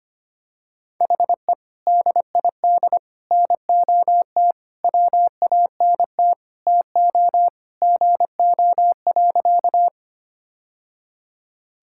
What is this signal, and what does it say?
Morse code, heard as a keyed tone, about 25 wpm, 700 Hz. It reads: HE DID NOT WANT TO GO.
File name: practice / 25wpm / Sentences with 3-Word Phrases - 25wpm